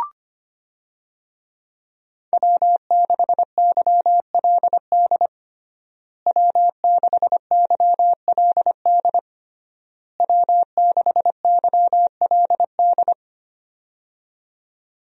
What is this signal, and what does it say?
Morse code, heard as a keyed tone, about 25 wpm, 700 Hz. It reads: W6YLD W6YLD W6YLD
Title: USA Call Signs - 5 Characters - Repeated 3x - 25wpm